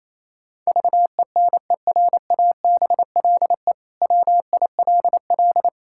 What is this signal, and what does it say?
Morse code, heard as a keyed tone, about 28 wpm, 700 Hz. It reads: VENERABLE WILL
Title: Pangram Challenge v2 - 28wpm